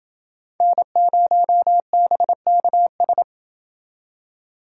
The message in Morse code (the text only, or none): N0BKS